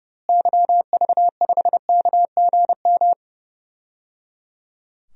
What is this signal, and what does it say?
Morse code, heard as a keyed tone, about 30 wpm, 700 Hz. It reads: YV5KGM
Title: DXCC-100 Call Signs - 6 Characters - Repeated 3x - 30wpm